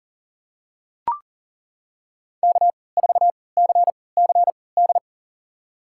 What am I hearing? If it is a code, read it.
Morse code, heard as a keyed tone, about 37 wpm, spaced as for 22 wpm, 700 Hz. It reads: K4CCD
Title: USA Call Signs - 5 Characters - 3x Character Spacing - 40wpm